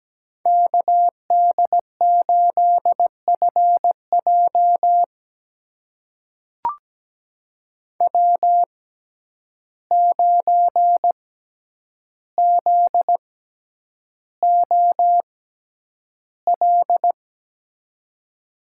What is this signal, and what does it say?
Morse code, heard as a keyed tone, about 17 wpm, spaced as for 9 wpm, 700 Hz. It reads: KD8FJ W9ZOL
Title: USA Call Signs - 5 Characters - 6x Character Spacing - 17wpm